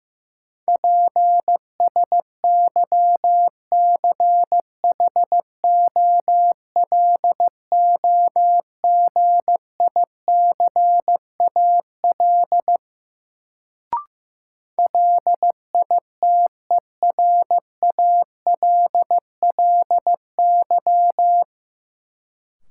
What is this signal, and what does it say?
Morse code, heard as a keyed tone, about 15 wpm, 700 Hz. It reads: PSYCHOLOGICAL LITERALLY
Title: Sentences from Top 1400 Words - New Word Drill - 15wpm